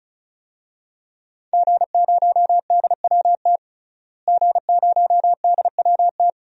Morse code, heard as a keyed tone, about 36 wpm, 700 Hz. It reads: G0DWT G0DWT
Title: DXCC-100 Call Signs - 5 Characters - Repeated 2x - 35wpm